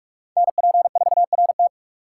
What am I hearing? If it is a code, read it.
Morse code, heard as a keyed tone, about 43 wpm, 700 Hz. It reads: NP4RT